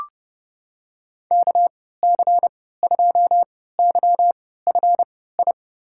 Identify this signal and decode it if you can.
Morse code, heard as a keyed tone, about 32 wpm, spaced as for 17 wpm, 700 Hz. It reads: KC2YFI
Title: USA Call Signs - 6 Characters - 3x Character Spacing - 30wpm